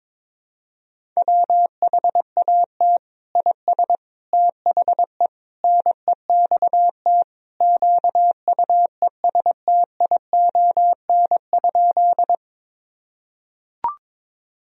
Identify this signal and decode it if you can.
Morse code, heard as a keyed tone, about 22 wpm, 700 Hz. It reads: WHAT IS THE NEXT QUESTION?
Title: Sentences from Top 1000 Words - Review Part 1 - 22wpm